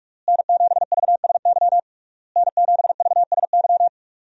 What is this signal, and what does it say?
Morse code, heard as a keyed tone, about 46 wpm, 700 Hz. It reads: N7VSY N7VSY
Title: USA Call Signs - 5 Characters - Repeated 3x - 45wpm